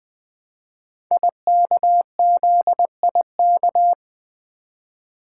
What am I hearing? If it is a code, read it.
Morse code, heard as a keyed tone, about 20 wpm, 700 Hz. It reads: IKZIK
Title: Code Groups - 5 Letters - 20wpm